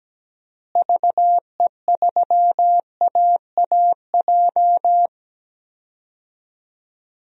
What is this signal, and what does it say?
Morse code, heard as a keyed tone, about 17 wpm, 700 Hz. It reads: VE3AAJ
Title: DXCC-100 Call Signs - 6 Characters - Repeated 3x - 17wpm